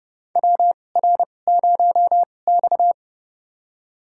Morse code, heard as a keyed tone, about 30 wpm, spaced as for 22 wpm, 700 Hz. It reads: WR0X
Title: USA Call Signs - 4 Characters - 2x Character Spacing - 30wpm